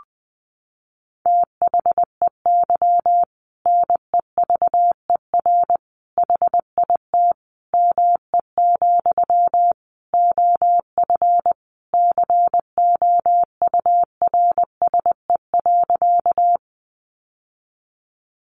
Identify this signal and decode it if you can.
Morse code, heard as a keyed tone, about 20 wpm, 700 Hz. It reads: THEY NEVER HIT ME, OF COURSE.
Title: Sentences from Top 1200 Words 20wpm